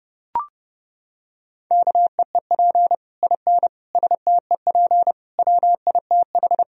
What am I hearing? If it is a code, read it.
Morse code, heard as a keyed tone, about 30 wpm, 700 Hz. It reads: KEEP IN STEP WITH
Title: Sentences from Top 1000 Words - Review Part 2 - 30wpm